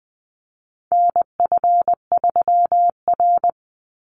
Morse code, heard as a keyed tone, about 20 wpm, 700 Hz. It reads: NF3R
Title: Call Signs Repeated 2x - 20wpm